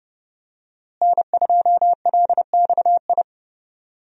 Morse code, heard as a keyed tone, about 32 wpm, 700 Hz. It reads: N2LXI
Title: USA Call Signs - 5 Characters - Repeated 3x - 30wpm